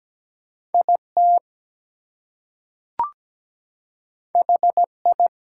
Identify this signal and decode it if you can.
Morse code, heard as a keyed tone, about 17 wpm, 700 Hz. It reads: IT HI
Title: Two Letter Words - 17wpm